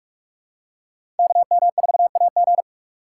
Morse code, heard as a keyed tone, about 46 wpm, 700 Hz. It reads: KM4AG
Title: USA Call Signs - 2x Character Spacing - 45wpm